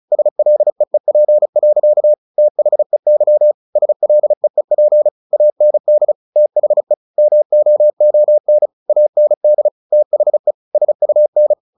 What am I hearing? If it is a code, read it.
Morse code, heard as a keyed tone, about 34 wpm, 600 Hz. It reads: SLEEP. THEY SLEEP AND THE MOON AND THE SUN
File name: the old man and the sea 2 35wpm